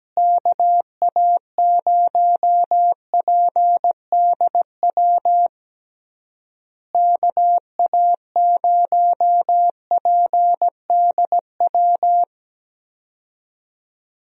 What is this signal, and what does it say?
Morse code, heard as a keyed tone, about 17 wpm, 700 Hz. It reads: KA0PDW KA0PDW
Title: USA Call Signs - 6 Characters - Repeated 3x - 17wpm